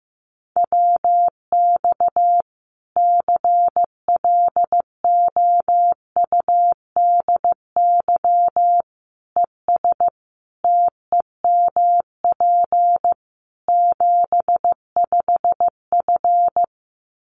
Morse code, heard as a keyed tone, about 15 wpm, 700 Hz. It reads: WX CLOUDY ES TEMP 75F
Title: Common QSO Phrases 15wpm